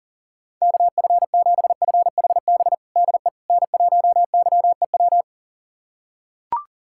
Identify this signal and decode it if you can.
Morse code, heard as a keyed tone, about 40 wpm, 700 Hz. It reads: KF7FHB DE N1YEW
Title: USA Callsign DE Callsign 40wpm